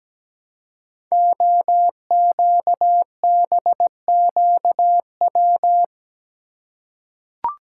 Morse code, heard as a keyed tone, about 17 wpm, 700 Hz. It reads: OQBQW
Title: Code Groups - 5 Letters - 17wpm